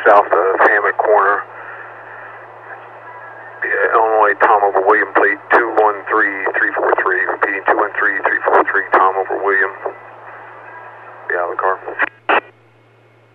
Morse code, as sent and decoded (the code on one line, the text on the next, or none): none